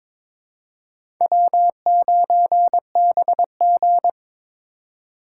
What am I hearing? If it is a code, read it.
Morse code, heard as a keyed tone, about 22 wpm, 700 Hz. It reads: W9BG
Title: USA Call Signs - 4x Character Spacing - 22wpm